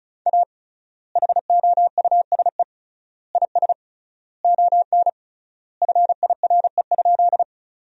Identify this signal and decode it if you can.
Morse code, heard as a keyed tone, about 34 wpm, 700 Hz. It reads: A HOUSE IS ON FIRE?
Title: 3x Word-Spacing - Sentences from Top 400 Words - 35wpm